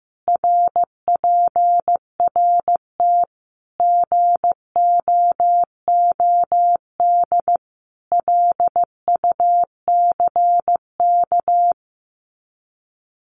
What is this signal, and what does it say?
Morse code, heard as a keyed tone, about 15 wpm, 700 Hz. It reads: RPRT GOOD LUCK
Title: Common QSO Phrases 15wpm